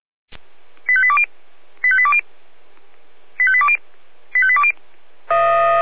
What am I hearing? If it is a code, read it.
Five-tone selective call (ZVEI), heard as quick runs of five tones, tones of about 70 ms. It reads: 85820 85820 85820 85820
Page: ZVEI